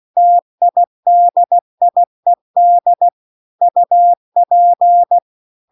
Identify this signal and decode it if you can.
Morse code, heard as a keyed tone, about 16 wpm, 700 Hz. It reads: TIDIED UP